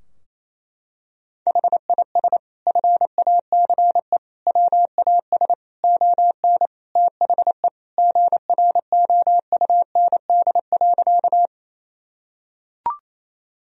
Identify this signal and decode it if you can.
Morse code, heard as a keyed tone, about 28 wpm, 700 Hz. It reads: HIS FACE WAS ON THE GROUND.
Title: Sentences from Top 1000 Words - Review Part 2 - 28wpm